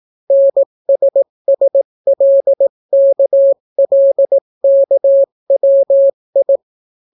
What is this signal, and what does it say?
Morse code, heard as a keyed tone, about 18 wpm, 550 Hz. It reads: NSSLKLKWI